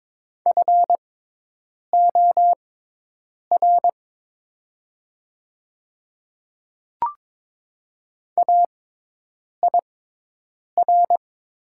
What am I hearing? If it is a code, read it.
Morse code, heard as a keyed tone, about 22 wpm, spaced as for 8 wpm, 700 Hz. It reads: FOR AIR
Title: 6x Character Spacing - Top 300 Words - 22wpm